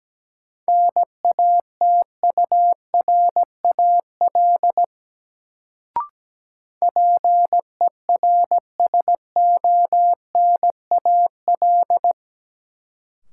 Morse code, heard as a keyed tone, about 17 wpm, 700 Hz. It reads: NATURAL PERSONAL
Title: Sentences from Top 700 Words - New Word Drill - 17wpm